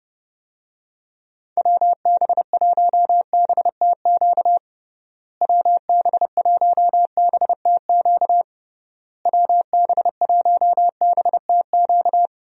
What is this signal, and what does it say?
Morse code, heard as a keyed tone, about 30 wpm, 700 Hz. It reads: WB1BTQ WB1BTQ WB1BTQ
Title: USA Call Signs - 6 Characters - Repeated 3x - 30wpm